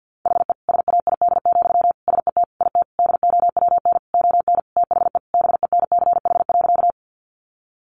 Morse code, heard as a keyed tone, about 50 wpm, 700 Hz. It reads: HE SAID, SET IT DOWN ON THE BENCH.
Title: Sentences from Top 1400 Words 50wpm